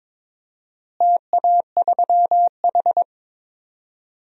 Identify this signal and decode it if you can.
Morse code, heard as a keyed tone, about 22 wpm, 700 Hz. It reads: TA3H
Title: DXCC-100 Call Signs - 4 Characters - Repeated 2x - 22wpm